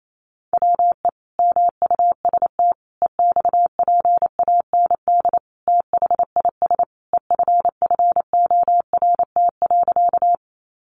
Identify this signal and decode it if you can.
Morse code, heard as a keyed tone, about 28 wpm, 700 Hz. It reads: WE MUST EXPAND THIS EFFORT.
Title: Sentences from Top 1400 Words 28wpm